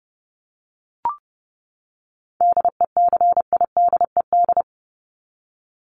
Morse code, heard as a keyed tone, about 30 wpm, 700 Hz. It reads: DECIDED
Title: ED Suffix - Top 100 Words with Base Words 30wpm